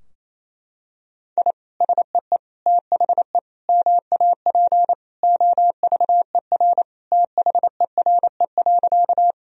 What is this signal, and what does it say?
Morse code, heard as a keyed tone, about 28 wpm, 700 Hz. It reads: I SEE THE MAP OVER THERE.
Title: Sentences from Top 1000 Words - Review Part 2 - 28wpm